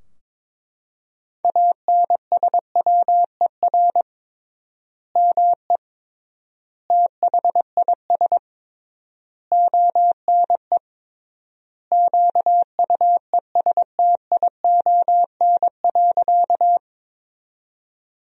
Morse code, heard as a keyed tone, about 22 wpm, 700 Hz. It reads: ANSWER ME THIS ONE QUESTION.
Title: 3x Word-Spacing - Sentences from Top 400 Words - 22wpm